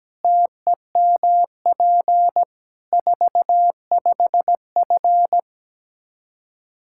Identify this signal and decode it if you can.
Morse code, heard as a keyed tone, about 17 wpm, 700 Hz. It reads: TEMP 45F
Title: Practice QSO - Exchange 2 - Info - 17wpm